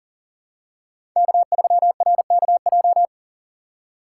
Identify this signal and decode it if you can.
Morse code, heard as a keyed tone, about 40 wpm, 700 Hz. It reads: K3RKJ
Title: USA Call Signs - 5 Characters - 2x Character Spacing - 40wpm